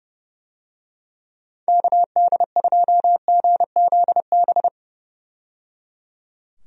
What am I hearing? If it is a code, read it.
Morse code, heard as a keyed tone, about 30 wpm, 700 Hz. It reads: KD2GZB
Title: USA Call Signs - 6 Characters - 30wpm